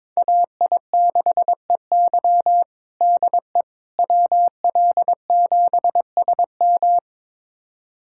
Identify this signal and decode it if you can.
Morse code, heard as a keyed tone, about 22 wpm, 700 Hz. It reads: AI6EY DE WL7SM
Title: USA Callsign DE Callsign - 5 Characters - 22wpm